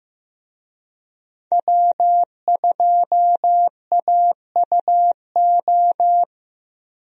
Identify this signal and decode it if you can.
Morse code, heard as a keyed tone, about 15 wpm, 700 Hz. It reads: W2AUO
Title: USA Call Signs - 5 Characters - Repeated 3x - 15wpm